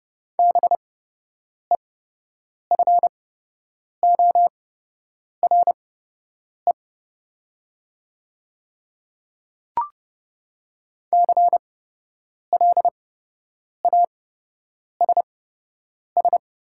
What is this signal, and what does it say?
Morse code, heard as a keyed tone, about 31 wpm, spaced as for 8 wpm, 700 Hz. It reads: BEFORE CLASS
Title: 8x Character Spacing - Top 500 Words - 30wpm